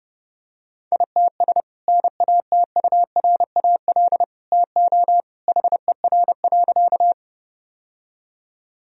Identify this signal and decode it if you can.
Morse code, heard as a keyed tone, about 30 wpm, 700 Hz. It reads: ITS NATURAL TO HER.
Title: Sentences from Top 1000 Words - Review Part 4 - 30wpm